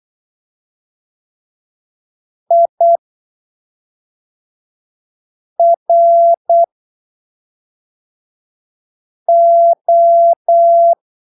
Morse code, heard as a keyed tone, about 8 wpm, spaced as for 3 wpm, 680 Hz. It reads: IRO
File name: arquivos / texto / texto 16 08ppm